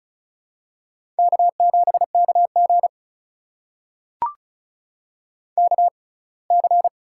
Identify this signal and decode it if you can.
Morse code, heard as a keyed tone, about 34 wpm, spaced as for 23 wpm, 700 Hz. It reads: K7KG KC
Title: USA Call Signs - 6x Character Spacing - 35wpm